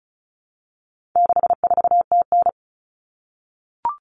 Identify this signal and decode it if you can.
Morse code, heard as a keyed tone, about 34 wpm, 700 Hz. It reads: 64TN
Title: All Cut Numbers - 4-Digit 35wpm